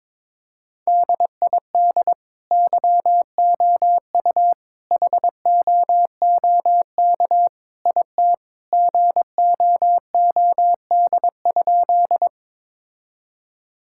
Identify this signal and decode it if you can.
Morse code, heard as a keyed tone, about 22 wpm, 700 Hz. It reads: DID YOU HOOK IT GOOD?
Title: Sentences from Top 1600 Words 22wpm